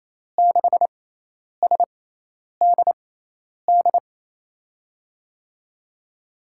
Morse code, heard as a keyed tone, about 27 wpm, spaced as for 10 wpm, 700 Hz. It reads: 6SDD
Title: USA Call Signs - 6 Characters - 6x Character Spacing - 28wpm